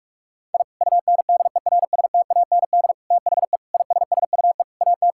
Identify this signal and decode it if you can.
Morse code, heard as a keyed tone, about 41 wpm, 700 Hz. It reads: I UNDERSTAND THE ISSUE AT
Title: Sentences from Top 1000 Words - Review Part 6 - 45wpm